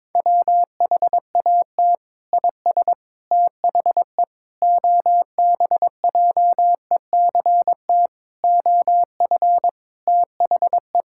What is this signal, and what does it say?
Morse code, heard as a keyed tone, about 22 wpm, 700 Hz. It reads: WHAT IS THE OBJECT OF THE
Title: Sentences from Top 1000 Words - Review Part 5 - 22wpm